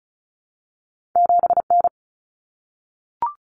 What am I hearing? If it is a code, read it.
Morse code, heard as a keyed tone, about 36 wpm, 700 Hz. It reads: 7N
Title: Common Cut Numbers - 2-Digit 35wpm